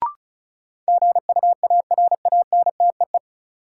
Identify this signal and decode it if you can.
Morse code, heard as a keyed tone, about 36 wpm, 700 Hz. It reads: GUARANTEE
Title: Sentences from Top 1600 Words - New Word Drill Alt - 35wpm